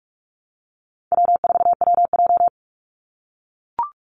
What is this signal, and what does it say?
Morse code, heard as a keyed tone, about 44 wpm, 700 Hz. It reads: W4WJ
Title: Easy Call Signs - 45wpm